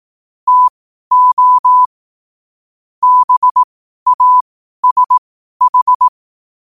Morse code, heard as a keyed tone, about 18 wpm, spaced as for 12 wpm, 1000 Hz. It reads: TO BASH
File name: qotdfiles / 12wpm